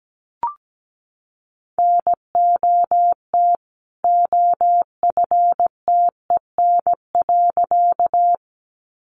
Morse code, heard as a keyed tone, about 17 wpm, 700 Hz. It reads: NOT OFTEN.
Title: A Sound Of Thunder 17wpm